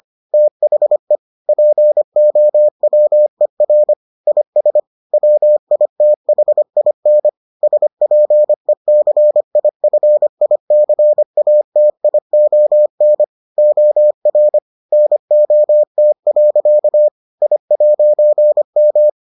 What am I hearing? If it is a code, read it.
Morse code, heard as a keyed tone, about 25 wpm, 600 Hz. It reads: THE POWER IS WITHIN SPECIFICATION OR NOT. I'M